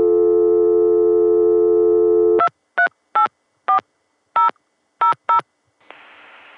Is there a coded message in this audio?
Yes: DTMF from 2.4 to 5.4 s, heard as telephone touch tones, tones of about 100 ms. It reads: 2384000